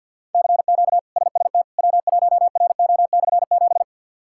Morse code, heard as a keyed tone, about 52 wpm, 690 Hz. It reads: CQ SST W1RO/7
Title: K1USN SST QSO Practice - 50wpm